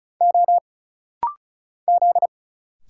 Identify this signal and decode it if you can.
Morse code, heard as a keyed tone, about 36 wpm, 700 Hz. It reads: O Z
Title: Single Letters - Rapid-Fire 35wpm